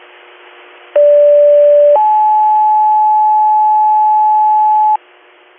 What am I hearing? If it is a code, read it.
Two-tone page, heard as a one-second tone, then a three-second tone, about 580 then 860 Hz, with noise. It is heard from 1.0 to 5.0 s.